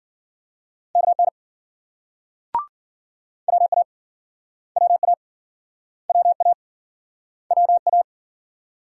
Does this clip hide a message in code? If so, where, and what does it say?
Yes: Morse code from 1.0 to 8.0 s, heard as a keyed tone, about 54 wpm, 700 Hz. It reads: KN WA WA WA WA